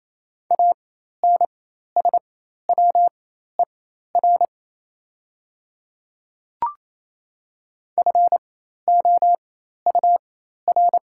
Morse code, heard as a keyed tone, about 28 wpm, spaced as for 13 wpm, 700 Hz. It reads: ANSWER FOUR
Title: 4x Character Spacing - Top 300 Words - 28wpm